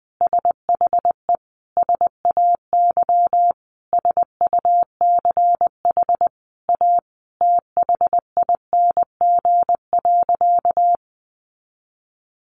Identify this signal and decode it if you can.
Morse code, heard as a keyed tone, about 20 wpm, 700 Hz. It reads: SHE SAY SUCH A THING.
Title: Sentences from Top 200 Words 20wpm